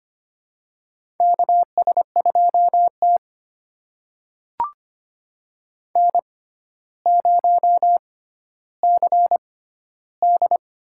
Morse code, heard as a keyed tone, about 25 wpm, spaced as for 13 wpm, 700 Hz. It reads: KS2T N0CD
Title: USA Call Signs - 4 Characters - 6x Character Spacing - 25wpm